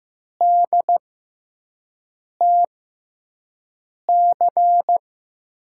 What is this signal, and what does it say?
Morse code, heard as a keyed tone, about 15 wpm, spaced as for 5 wpm, 700 Hz. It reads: DTC
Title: USA Call Signs - 6x Character Spacing - 15wpm